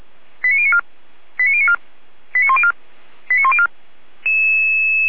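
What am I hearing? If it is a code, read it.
Five-tone selective call (ZVEI), heard as quick runs of five tones, tones of about 70 ms. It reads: 89094 89094 89104 89104
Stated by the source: ZVEI five-tone call 89276 received by SDR, from the Feuerwehr Orlamünde audio server.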